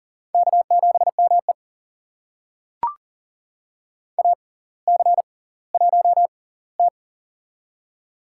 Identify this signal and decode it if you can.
Morse code, heard as a keyed tone, about 40 wpm, spaced as for 20 wpm, 700 Hz. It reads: K7ME AC1T
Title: USA Call Signs - 4 Characters - 6x Character Spacing - 40wpm